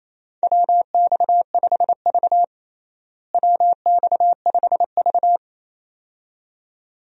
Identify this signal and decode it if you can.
Morse code, heard as a keyed tone, about 28 wpm, 700 Hz. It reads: WX5V WX5V
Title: USA Call Signs - 4 Characters - Repeated 2x - 28wpm